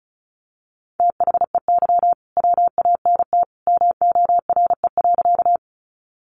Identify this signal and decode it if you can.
Morse code, heard as a keyed tone, about 34 wpm, 700 Hz. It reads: THEY WANT MORE.
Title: Sentences from Top 200 Words 35wpm